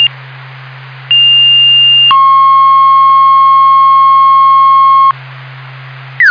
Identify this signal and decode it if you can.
Two-tone page, heard as a one-second tone, then a three-second tone, about 2800 then 1080 Hz. It is heard from 1.1 to 5.1 s.